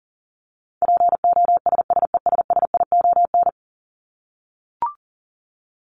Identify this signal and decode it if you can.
Morse code, heard as a keyed tone, about 40 wpm, 700 Hz. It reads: POSSESSION
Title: Top 1251-1500 Words 40wpm